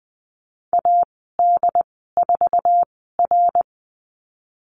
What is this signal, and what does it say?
Morse code, heard as a keyed tone, about 20 wpm, spaced as for 14 wpm, 700 Hz. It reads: AD4R
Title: 082 Lesson 14g - Number 2 - Call Signs 20wpm